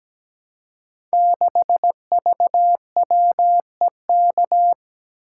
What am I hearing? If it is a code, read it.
Morse code, heard as a keyed tone, about 17 wpm, 700 Hz. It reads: 6VWEK
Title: Code Groups - 5 Letter:Numbers - 17wpm